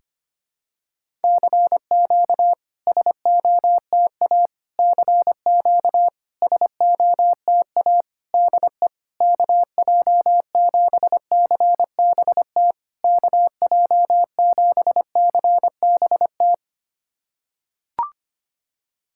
Morse code, heard as a keyed tone, about 25 wpm, 700 Hz. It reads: CQ SOTA CQ SOTA DE KJ7CBT KJ7CBT
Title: SOTA QSO Practice 25wpm